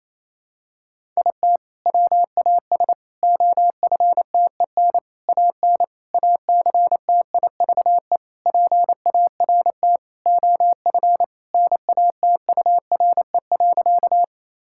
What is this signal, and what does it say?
Morse code, heard as a keyed tone, about 28 wpm, 700 Hz. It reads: IT WAS OFTEN AN ACTIVE PART OF NATURE.
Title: Sentences from Top 1000 Words - Review Part 6 - 28wpm